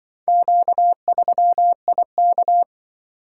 Morse code, heard as a keyed tone, about 25 wpm, 700 Hz. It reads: Q3IK
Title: Speed-Racing - Four Letter-Number Combinations - 20wpm